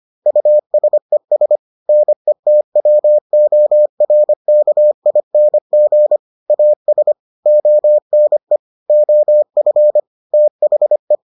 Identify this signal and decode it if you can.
Morse code, heard as a keyed tone, about 25 wpm, 600 Hz. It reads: USES NETWORKING AS ONE OF THE